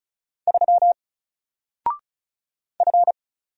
Morse code, heard as a keyed tone, about 36 wpm, 700 Hz. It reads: 3 F
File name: Single-Letter-Number-35wpm